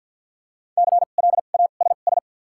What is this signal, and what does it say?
Morse code, heard as a keyed tone, about 54 wpm, spaced as for 37 wpm, 700 Hz. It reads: CLASS
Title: 2x Character Spacing - Top 500 Words - 50wpm